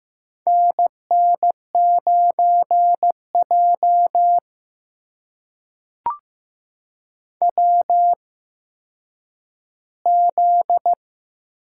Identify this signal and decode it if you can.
Morse code, heard as a keyed tone, about 15 wpm, spaced as for 9 wpm, 700 Hz. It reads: NN9J WZ
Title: USA Call Signs - 4 Characters - 8x Character Spacing - 15wpm